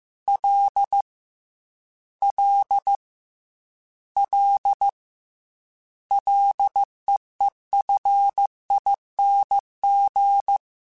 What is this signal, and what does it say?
Morse code, heard as a keyed tone, about 15 wpm, 800 Hz. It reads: L L L LEEFING
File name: Ls / Single L m3s / Single L @13